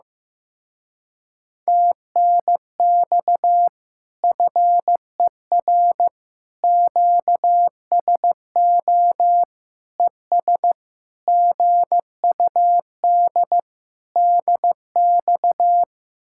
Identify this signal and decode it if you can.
Morse code, heard as a keyed tone, about 15 wpm, 700 Hz. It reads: TNX FER QSO ES GUD DX
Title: Speed-Racing Common QSO Phrases Encore - 15wpm